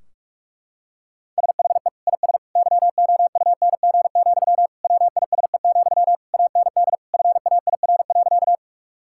Morse code, heard as a keyed tone, about 46 wpm, 700 Hz. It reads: SHE IS YOUNG, WISE, AND FAIR.
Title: Sentences from Top 1600 Words 45wpm